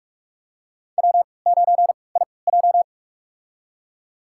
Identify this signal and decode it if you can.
Morse code, heard as a keyed tone, about 46 wpm, spaced as for 25 wpm, 700 Hz. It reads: W9IJ
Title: USA Call Signs - 4 Characters - 3x Character Spacing - 45wpm